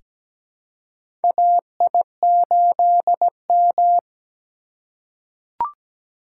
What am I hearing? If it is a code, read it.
Morse code, heard as a keyed tone, about 17 wpm, 700 Hz. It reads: AI8M
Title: USA Call Signs - 3x Character Spacing - 17wpm